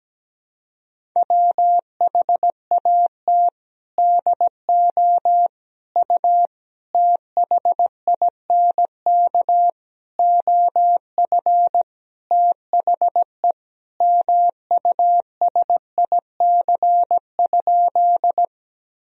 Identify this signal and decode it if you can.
Morse code, heard as a keyed tone, about 17 wpm, 700 Hz. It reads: WHAT DO U THINK OF THE MUSIC?
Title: Sentences from Top 300 Words and Abbreviations 17wpm